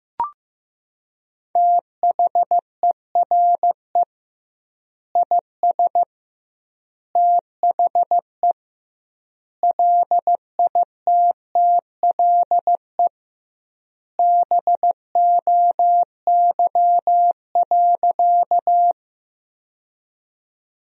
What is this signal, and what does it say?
Morse code, heard as a keyed tone, about 15 wpm, 700 Hz. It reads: THERE IS THE LITTLE BOY.
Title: 2x Word-Spacing - Sentences from Top 200 Words - 15wpm